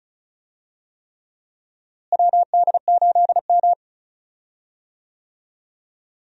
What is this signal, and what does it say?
Morse code, heard as a keyed tone, about 36 wpm, 700 Hz. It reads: WD8M
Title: USA Call Signs - 4 Characters - Repeated 3x - 35wpm